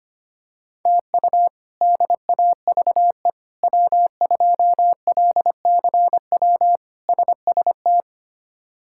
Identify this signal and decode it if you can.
Morse code, heard as a keyed tone, about 25 wpm, 700 Hz. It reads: TU DAVE W2LCW SST
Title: K1USN SST QSO Practice - 25wpm